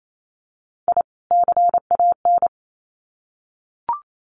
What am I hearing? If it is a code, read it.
Morse code, heard as a keyed tone, about 28 wpm, 700 Hz. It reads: I CAN
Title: Sets of 2 Words 28wpm